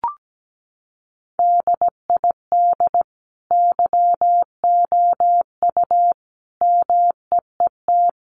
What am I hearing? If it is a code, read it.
Morse code, heard as a keyed tone, about 17 wpm, 700 Hz. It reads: DID YOU MEET
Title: Sentences from Top 1100 Words 17wpm